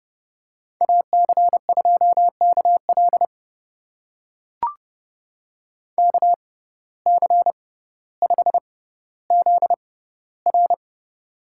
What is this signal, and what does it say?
Morse code, heard as a keyed tone, about 30 wpm, spaced as for 15 wpm, 700 Hz. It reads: AC2KL KC5ZR
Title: USA Call Signs - 5 Characters - 6x Character Spacing - 30wpm